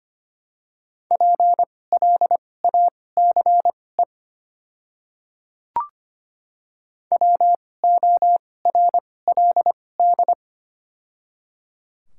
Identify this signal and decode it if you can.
Morse code, heard as a keyed tone, about 25 wpm, spaced as for 18 wpm, 700 Hz. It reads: PLACE WORLD
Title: 2x Character Spacing - Top 200 Words - 25wpm